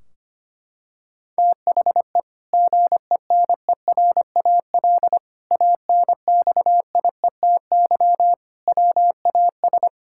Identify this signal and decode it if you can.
Morse code, heard as a keyed tone, about 25 wpm, 700 Hz. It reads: THE GENERAL ANXIETY WAS
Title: Sentences from Top 1600 Words 25wpm